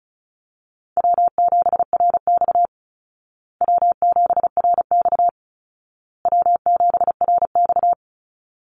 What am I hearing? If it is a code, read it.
Morse code, heard as a keyed tone, about 34 wpm, 700 Hz. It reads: W7RX W7RX W7RX
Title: Call Signs Repeated 2x - 35wpm